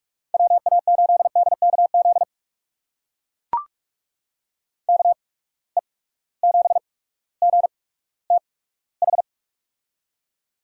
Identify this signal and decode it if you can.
Morse code, heard as a keyed tone, about 42 wpm, spaced as for 19 wpm, 700 Hz. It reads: WA8DKZ KE7GTH
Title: USA Call Signs - 6 Characters - 8x Character Spacing - 45wpm